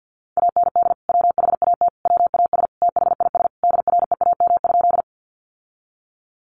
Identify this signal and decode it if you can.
Morse code, heard as a keyed tone, about 50 wpm, 700 Hz. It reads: AND WHAT WAS THIS DREAM?
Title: Sentences from Top 1500 Words 50wpm